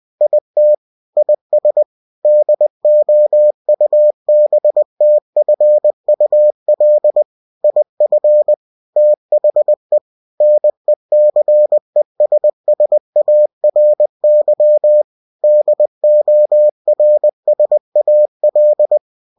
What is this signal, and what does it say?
Morse code, heard as a keyed tone, about 20 wpm, 600 Hz. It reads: IT IS DOUBTFUL IF THE NECESSARY DORSAL
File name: cw-books / winnie-the-pooh / winnie-20wpm-page-0098